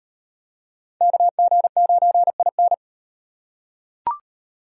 Morse code, heard as a keyed tone, about 37 wpm, 700 Hz. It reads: KG9IN